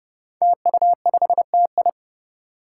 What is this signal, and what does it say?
Morse code, heard as a keyed tone, about 30 wpm, 700 Hz. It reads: TU5TI
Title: Five Challenging Letter-Number Combinations 30wpm